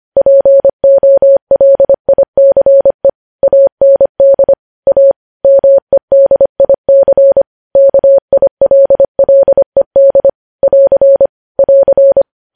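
Morse code, heard as a keyed tone, about 25 wpm, 550 Hz. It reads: POLICE AND A MEDIC KILLED <AR> <AR>
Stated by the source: News headlines sent in Morse code at 25.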